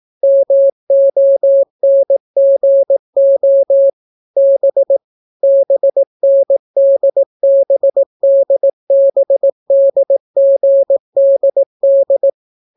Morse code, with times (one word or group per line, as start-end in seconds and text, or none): 0.2-3.9 MONGO
4.4-5.0 B
5.4-12.3 BNDBDBDGDD